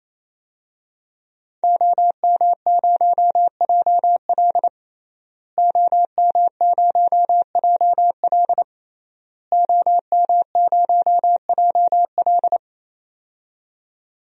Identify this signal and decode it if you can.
Morse code, heard as a keyed tone, about 28 wpm, 700 Hz. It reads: OM0JL OM0JL OM0JL
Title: DXCC-100 Call Signs - 5 Characters - Repeated 3x - 28wpm